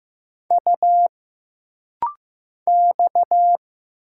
Morse code, heard as a keyed tone, about 15 wpm, 700 Hz. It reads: U X
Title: Single Letters - Rapid-Fire 15wpm